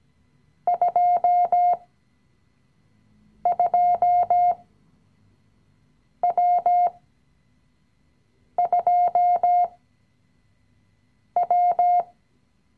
Morse code, heard as a keyed tone, about 17 wpm, spaced as for 5 wpm, 710 Hz. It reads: 22W2W
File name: CW-Online-Kurs / Anfänger / Lektion 017